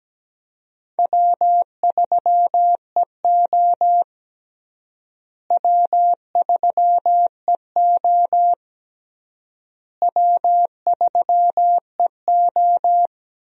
Morse code, heard as a keyed tone, about 17 wpm, 700 Hz. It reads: W3EO W3EO W3EO
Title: USA Call Signs - 4 Characters - Repeated 3x - 17wpm